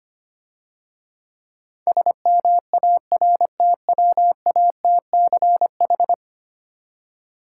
Morse code, heard as a keyed tone, about 28 wpm, 700 Hz. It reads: SMARTWATCH
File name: practice / 25wpm / Difficult Words - Speed-Racing with Hints - 25wpm